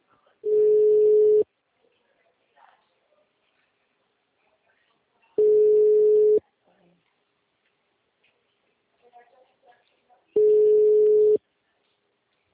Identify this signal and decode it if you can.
Ringback tone (Europe, 425 Hz), with speech.
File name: out- -2033- - - .36043